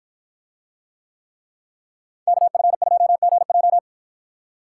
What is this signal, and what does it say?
Morse code, heard as a keyed tone, about 49 wpm, 700 Hz. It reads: KF2GJ